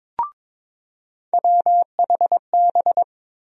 Morse code, heard as a keyed tone, about 22 wpm, 700 Hz. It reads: WHB
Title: USA Call Sign Postfixes - 3 Letters 22wpm